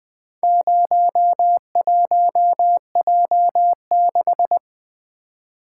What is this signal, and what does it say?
Morse code, heard as a keyed tone, about 20 wpm, 700 Hz. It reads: 01J6
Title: Four Letter-Number Combinations - 20wpm